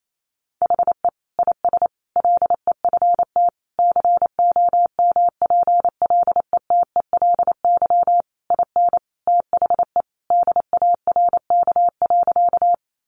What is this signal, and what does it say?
Morse code, heard as a keyed tone, about 28 wpm, 700 Hz. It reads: HE IS LEFT COMPLETELY IN THE DARK.
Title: Sentences from Top 1100 Words 28wpm